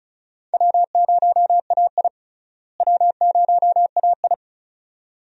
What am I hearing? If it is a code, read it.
Morse code, heard as a keyed tone, about 34 wpm, 700 Hz. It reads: W0AI W0AI
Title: USA Call Signs - 4 Characters - Repeated 2x - 35wpm